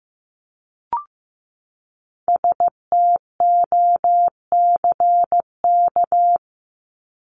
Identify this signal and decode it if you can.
Morse code, heard as a keyed tone, about 15 wpm, 700 Hz. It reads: STOCK
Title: Top 501-750 Words 15wpm